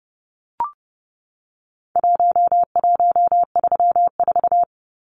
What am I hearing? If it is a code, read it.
Morse code, heard as a keyed tone, about 30 wpm, 700 Hz. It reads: 1134
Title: Numbers - Contest Style - 4-Digit 30wpm